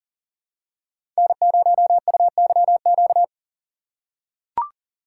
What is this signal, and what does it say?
Morse code, heard as a keyed tone, about 40 wpm, 700 Hz. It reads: N0UYQ
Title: USA Call Signs - 5 Characters - 4x Character Spacing - 40wpm